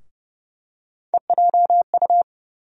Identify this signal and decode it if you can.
Morse code, heard as a keyed tone, about 30 wpm, 700 Hz. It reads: EJU
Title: USA Call Sign Postfixes - 3 Letters 30wpm